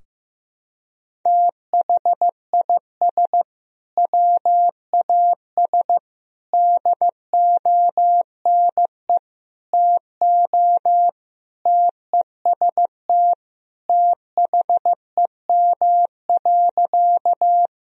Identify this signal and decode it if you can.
Morse code, heard as a keyed tone, about 15 wpm, 700 Hz. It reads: THIS WAS DONE TO TEST THEM.
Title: Sentences from Top 1000 Words - Review Part 3 - 15wpm